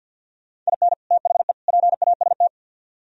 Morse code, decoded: IN THE PAST

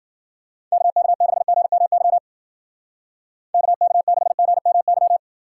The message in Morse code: KK6GMY KK6GMY

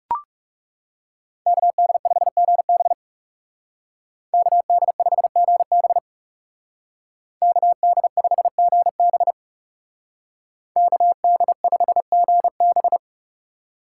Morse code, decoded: KD5GB KD5GB KD5GB KD5GB

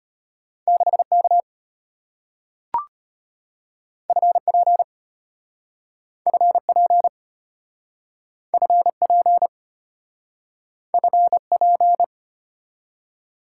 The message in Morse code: 6K FP FP FP FP

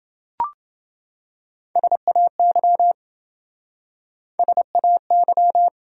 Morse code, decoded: SAY SAY

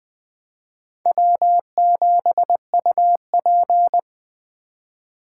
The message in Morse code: W7UP

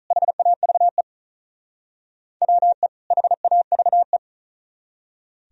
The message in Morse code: HAVE WE HAVE

